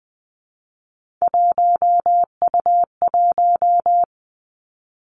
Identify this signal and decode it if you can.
Morse code, heard as a keyed tone, about 20 wpm, 700 Hz. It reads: 1U1
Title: Three Challenging Letter-Number Combinations 20wpm